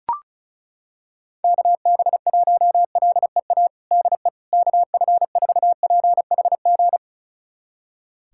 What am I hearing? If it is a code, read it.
Morse code, heard as a keyed tone, about 36 wpm, 700 Hz. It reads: KB1LEA DE KF4PHG
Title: USA Callsign DE Callsign 35wpm